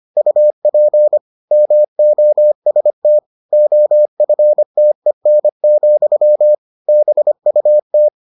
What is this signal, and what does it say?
Morse code, heard as a keyed tone, about 25 wpm, 600 Hz. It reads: UP MOST OFTEN, BUT